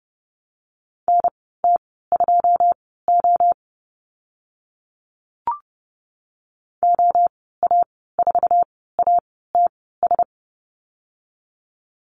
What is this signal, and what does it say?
Morse code, heard as a keyed tone, about 30 wpm, spaced as for 17 wpm, 700 Hz. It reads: NT2O OA4ATS
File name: Lesson 14g - Number 2 - Call Signs 30wpm